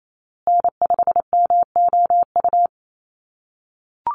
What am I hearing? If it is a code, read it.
Morse code, heard as a keyed tone, about 28 wpm, 700 Hz. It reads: N5MOU
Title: USA Call Signs Repeated 3x 28wpm